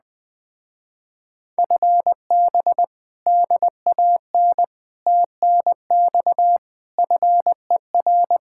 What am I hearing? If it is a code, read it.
Morse code, heard as a keyed tone, about 20 wpm, 700 Hz. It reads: FB DAN TNX FER
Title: Speed-Racing Common QSO Phrases Encore - 17wpm